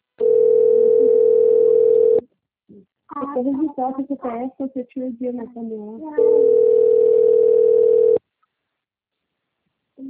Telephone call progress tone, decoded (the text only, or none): ringback tone